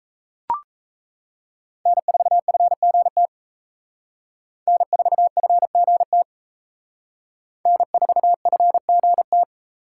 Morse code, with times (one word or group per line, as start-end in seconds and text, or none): 1.9-3.3 N4FGT
4.7-6.2 N4FGT
7.7-9.4 N4FGT